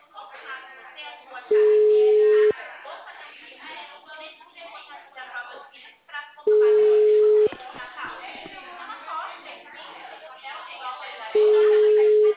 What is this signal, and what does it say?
Ringback tone (Europe, 425 Hz), with speech.